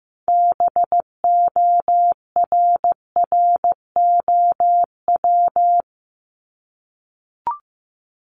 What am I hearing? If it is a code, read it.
Morse code, heard as a keyed tone, about 15 wpm, 700 Hz. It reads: BORROW
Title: Top 1251-1500 Words 15wpm